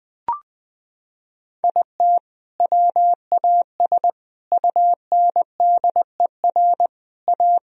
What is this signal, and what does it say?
Morse code, heard as a keyed tone, about 20 wpm, 700 Hz. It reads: IT WAS UNDER A
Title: Sentences from Top 1000 Words - Review Part 1 - 20wpm